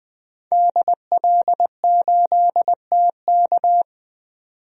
Morse code, heard as a keyed tone, about 21 wpm, 700 Hz. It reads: DL8TK
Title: DXCC-100 Call Signs - 5 Characters - 20wpm